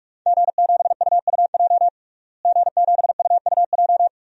Morse code, heard as a keyed tone, about 46 wpm, 700 Hz. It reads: G7UUJ G7UUJ
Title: DXCC-100 Call Signs - 5 Characters - Repeated 2x - 45wpm